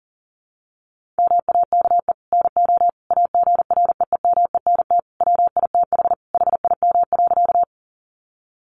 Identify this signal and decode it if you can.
Morse code, heard as a keyed tone, about 40 wpm, 700 Hz. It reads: MAKE NO AGREEMENT WITH HIM.
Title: Sentences from Top 1500 Words 40wpm